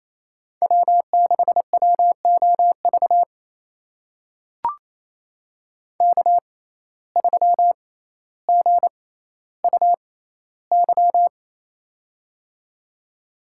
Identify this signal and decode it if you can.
Morse code, heard as a keyed tone, about 28 wpm, spaced as for 14 wpm, 700 Hz. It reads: W6WOV K3GUY
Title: USA Call Signs - 5 Characters - 6x Character Spacing - 28wpm